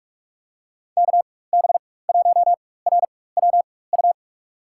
Morse code, heard as a keyed tone, about 45 wpm, spaced as for 21 wpm, 700 Hz. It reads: KB1RWU
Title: USA Call Signs - 6 Characters - 4x Character Spacing - 45wpm